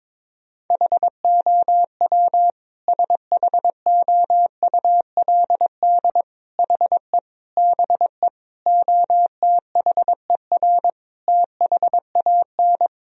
Morse code, heard as a keyed tone, about 22 wpm, 700 Hz. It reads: HOW SHOULD HE BE OTHER THAN
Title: Sentences from Top 1000 Words - Review Part 2 - 22wpm